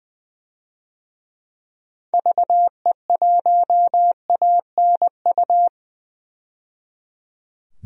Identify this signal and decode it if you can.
Morse code, heard as a keyed tone, about 20 wpm, 700 Hz. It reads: VE1ANU
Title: DXCC-100 Call Signs - 6 Characters - 20wpm